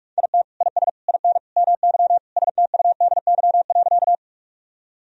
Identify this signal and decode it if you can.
Morse code, heard as a keyed tone, about 41 wpm, 700 Hz. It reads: IT IS IN MY STUDY.
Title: Sentences from Top 1000 Words - Review Part 1 - 45wpm